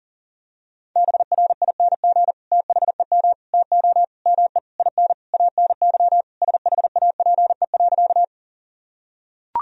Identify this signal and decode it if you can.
Morse code, heard as a keyed tone, about 37 wpm, 700 Hz. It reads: BRING THEM TO ME IN ANY SHAPE.
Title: Sentences from Top 1000 Words - Review Part 3 - 40wpm